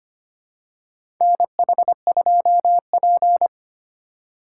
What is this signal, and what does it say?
Morse code, heard as a keyed tone, about 25 wpm, 700 Hz. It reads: NH2P
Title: USA Call Signs - 4 Characters - 2x Character Spacing - 25wpm